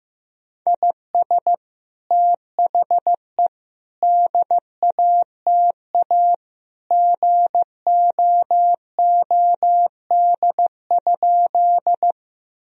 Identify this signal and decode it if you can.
Morse code, heard as a keyed tone, about 15 wpm, 700 Hz. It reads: IS THE DATA GOOD?
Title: Sentences from Top 1000 Words - Review Part 3 - 15wpm